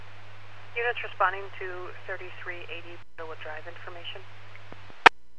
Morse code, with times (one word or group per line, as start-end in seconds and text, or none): none